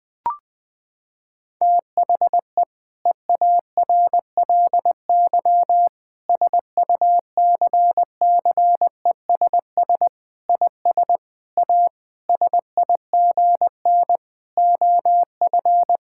Morse code, heard as a keyed tone, about 21 wpm, 700 Hz. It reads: THE EARLY SUCCESS IS A SIGN OF
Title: Sentences with 6-Word Phrases - 20wpm